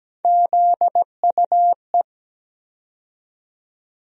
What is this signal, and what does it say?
Morse code, heard as a keyed tone, about 17 wpm, 700 Hz. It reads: ZUE
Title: USA Call Sign Postfixes - 3 Letters 17wpm